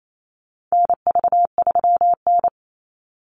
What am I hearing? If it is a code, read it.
Morse code, heard as a keyed tone, about 28 wpm, 700 Hz. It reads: NV3N
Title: Call Signs - 28wpm